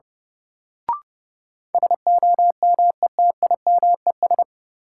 Morse code, heard as a keyed tone, about 30 wpm, 700 Hz. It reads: SOMETIMES
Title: Sentences from Top 600 Words - New Word Drill - 30wpm